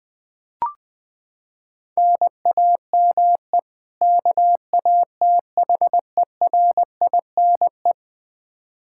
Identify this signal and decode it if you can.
Morse code, heard as a keyed tone, about 20 wpm, 700 Hz. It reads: NAME KATHERINE
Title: Practice QSO - Exchange 1 - Beginning - 20wpm